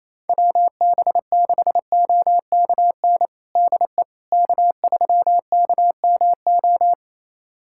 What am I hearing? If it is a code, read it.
Morse code, heard as a keyed tone, about 28 wpm, 700 Hz. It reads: WB6OKN DE K3KMO
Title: USA Callsign DE Callsign 28wpm